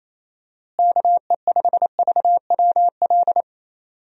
Morse code, heard as a keyed tone, about 28 wpm, 700 Hz. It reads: KE5VWL
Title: USA Call Signs - 6 Characters - Repeated 2x - 28wpm